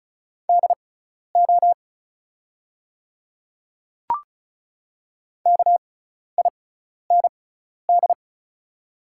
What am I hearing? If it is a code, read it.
Morse code, heard as a keyed tone, about 36 wpm, spaced as for 12 wpm, 700 Hz. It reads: DO KIND
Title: 6x Character Spacing - Top 200 Words - 35wpm